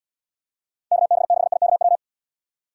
Morse code, heard as a keyed tone, about 74 wpm, 700 Hz. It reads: KB6IKW